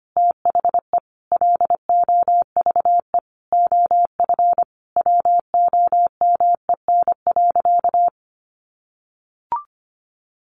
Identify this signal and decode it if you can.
Morse code, heard as a keyed tone, about 25 wpm, 700 Hz. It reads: THE LOVE OF WOMEN.
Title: Sentences from Top 1400 Words 25wpm